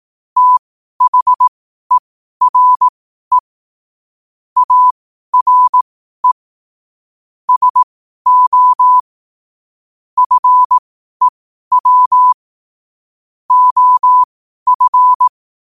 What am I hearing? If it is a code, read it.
Morse code, heard as a keyed tone, about 18 wpm, spaced as for 12 wpm, 1000 Hz. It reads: THERE ARE SO FEW OF
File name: qotdfiles / 12wpm